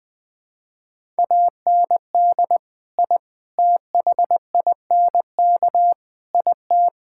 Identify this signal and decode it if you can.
Morse code, heard as a keyed tone, about 20 wpm, 700 Hz. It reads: AND I THINK IT
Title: Build A Phrase - 4-Word Phrases - 20wpm